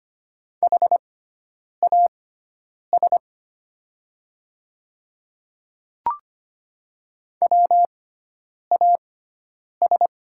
Morse code, heard as a keyed tone, about 25 wpm, spaced as for 9 wpm, 690 Hz. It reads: HAS WAS
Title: 6x Character Spacing - Top 100 Words - 25wpm